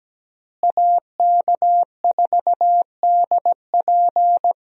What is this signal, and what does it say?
Morse code, heard as a keyed tone, about 17 wpm, 700 Hz. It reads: AK4DP
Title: USA Call Signs - 5 Characters - Repeated 3x - 17wpm